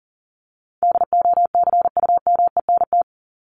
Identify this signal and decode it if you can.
Morse code, heard as a keyed tone, about 40 wpm, 700 Hz. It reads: DOCUMENT
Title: Top 1751-2000 Words 40wpm